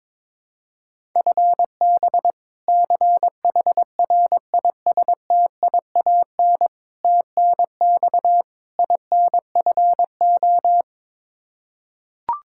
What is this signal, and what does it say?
Morse code, heard as a keyed tone, about 22 wpm, 700 Hz. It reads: FB CHRISTIAN TNX INFO
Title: K1USN SST QSO Practice 22wpm